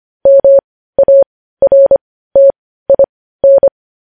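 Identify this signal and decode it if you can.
Morse code, heard as a keyed tone, about 25 wpm, spaced as for 15 wpm, 550 Hz. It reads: MARTIN
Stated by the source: Morse code news headlines sent at 15 wpm.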